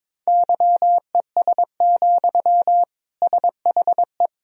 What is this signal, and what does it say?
Morse code, heard as a keyed tone, about 22 wpm, 700 Hz. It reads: YES, SHE
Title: Sentences from Top 500 Words and Abbreviations 22wpm